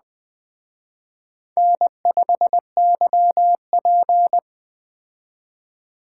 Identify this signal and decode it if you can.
Morse code, heard as a keyed tone, about 20 wpm, 700 Hz. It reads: N5YP
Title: Speed Racing - USA 1x2 Call Signs 17wpm